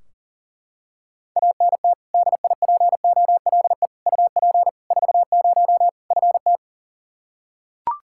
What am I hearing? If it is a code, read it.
Morse code, heard as a keyed tone, about 40 wpm, 700 Hz. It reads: ANT DIPOLE UP 40 FT
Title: K1USN SST QSO Practice 40wpm